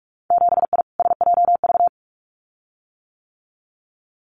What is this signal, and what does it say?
Morse code, heard as a keyed tone, about 46 wpm, 700 Hz. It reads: 7I SJV
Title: Winter Field Day Exchanges Encore 45wpm